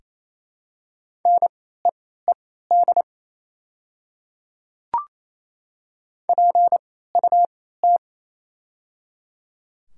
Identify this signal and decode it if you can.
Morse code, heard as a keyed tone, about 28 wpm, spaced as for 16 wpm, 700 Hz. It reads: NEED PUT
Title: 3x Character Spacing - Top 200 Words - 28wpm